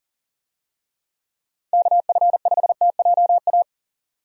Morse code, heard as a keyed tone, about 40 wpm, 700 Hz. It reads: KF5TJA
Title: USA Call Signs - 6 Characters - Repeated 2x - 40wpm